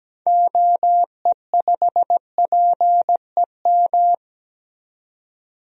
Morse code, heard as a keyed tone, about 17 wpm, 700 Hz. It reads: OE5PEM
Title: DXCC-100 Call Signs - 6 Characters - 17wpm